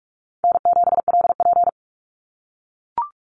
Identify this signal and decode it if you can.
Morse code, heard as a keyed tone, about 44 wpm, 700 Hz. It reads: N7LP
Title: NA SOTA Chaser Callsigns 45wpm